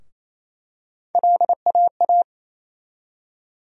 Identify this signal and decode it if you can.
Morse code, heard as a keyed tone, about 28 wpm, 700 Hz. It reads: LAA
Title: USA Call Sign Postfixes - 3 Letters 28wpm